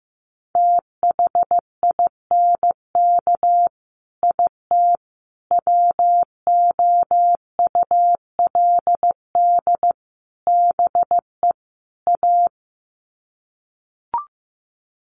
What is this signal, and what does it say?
Morse code, heard as a keyed tone, about 15 wpm, 700 Hz. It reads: THINK IT WOULD BE A
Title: Sets of 5 Words Encore2 15wpm